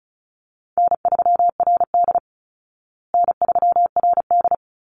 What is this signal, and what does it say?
Morse code, heard as a keyed tone, about 34 wpm, 700 Hz. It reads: N3RD N3RD
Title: Call Signs Repeated 3x - 35wpm